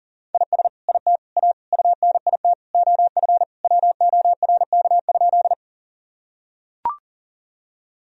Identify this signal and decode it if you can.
Morse code, heard as a keyed tone, about 40 wpm, 700 Hz. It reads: IS IT A UNIT OF WORK?
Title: Sentences from Top 1000 Words - Review Part 2 - 40wpm